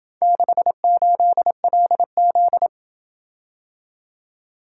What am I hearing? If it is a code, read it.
Morse code, heard as a keyed tone, about 27 wpm, 700 Hz. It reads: 68LZ